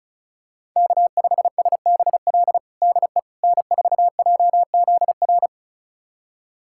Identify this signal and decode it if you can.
Morse code, heard as a keyed tone, about 36 wpm, 700 Hz. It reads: K5SBL DE N4JZR
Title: USA Callsign DE Callsign - 5 Characters - 35wpm